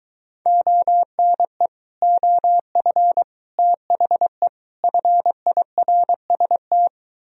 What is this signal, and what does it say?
Morse code, heard as a keyed tone, about 23 wpm, 700 Hz. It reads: ONE OF THE FIRST